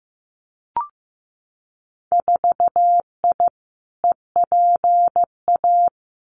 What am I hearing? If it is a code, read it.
Morse code, heard as a keyed tone, about 15 wpm, 700 Hz. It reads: 4I EPA
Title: Winter Field Day Exchanges 15wpm